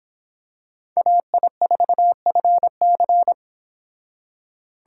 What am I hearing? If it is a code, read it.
Morse code, heard as a keyed tone, about 25 wpm, 700 Hz. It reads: AI4FC